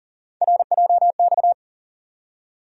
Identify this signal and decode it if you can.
Morse code, heard as a keyed tone, about 40 wpm, 700 Hz. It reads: RJX
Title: USA Call Sign Postfixes - 3 Letters 40wpm